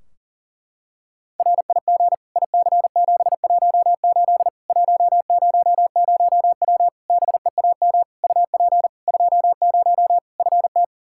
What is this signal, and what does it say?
Morse code, heard as a keyed tone, about 40 wpm, 700 Hz. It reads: RIG IC718 100W BEAM UP 20 FT
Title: K1USN SST QSO Practice 40wpm